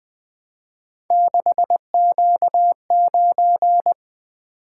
Morse code, heard as a keyed tone, about 20 wpm, 700 Hz. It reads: 6Q9
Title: Three Letter-Number Combinations - 20wpm